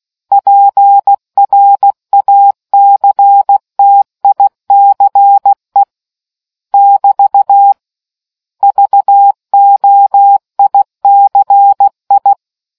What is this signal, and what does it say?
Morse code, heard as a keyed tone, about 16 wpm, 800 Hz. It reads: PRACTICE <BT> VOICI